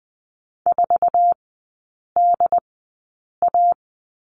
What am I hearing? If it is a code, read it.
Morse code, heard as a keyed tone, about 20 wpm, spaced as for 8 wpm, 700 Hz. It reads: 4DA
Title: 067 Lesson 12e - Letter d - Three Characters 20wpm